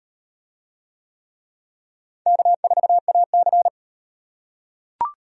Morse code, heard as a keyed tone, about 38 wpm, 700 Hz. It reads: K4AC